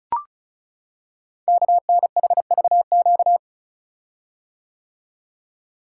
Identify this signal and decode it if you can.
Morse code, heard as a keyed tone, about 36 wpm, 690 Hz. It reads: KNHVQ
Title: Code Groups - 5 Letters - 35wpm